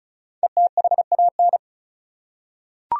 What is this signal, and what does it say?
Morse code, heard as a keyed tone, about 34 wpm, 700 Hz. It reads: ETHAN
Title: USA Common Names - 35wpm